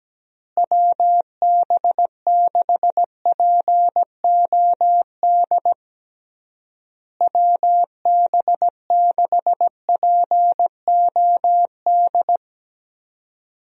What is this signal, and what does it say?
Morse code, heard as a keyed tone, about 17 wpm, 700 Hz. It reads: WB6POD WB6POD
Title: USA Call Signs - 6 Characters - Repeated 3x - 17wpm